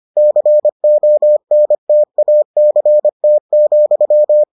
Morse code, heard as a keyed tone, about 25 wpm, 600 Hz. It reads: CONTACT,